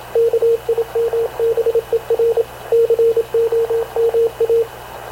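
Morse code, heard as a keyed tone, about 26 wpm, 460 Hz, with radio noise. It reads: KIMBER COMA
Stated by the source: Havana Radio CLA press (PX) broadcast, recorded by Radio Officer Paul Dumesnil aboard mv CNAV QUEST/CZDO.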